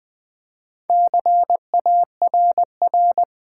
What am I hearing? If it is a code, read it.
Morse code, heard as a keyed tone, about 20 wpm, 700 Hz. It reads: CARR
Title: Build A Word - 4 Letters At A Time - 8-Letter Words - 20wpm